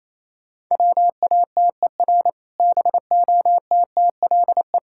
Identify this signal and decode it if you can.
Morse code, heard as a keyed tone, about 28 wpm, 700 Hz. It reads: WATER BOTTLE